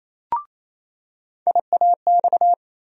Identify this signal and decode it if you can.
Morse code, heard as a keyed tone, about 28 wpm, 700 Hz. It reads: IAX
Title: USA Call Sign Postfixes - 3 Letters 28wpm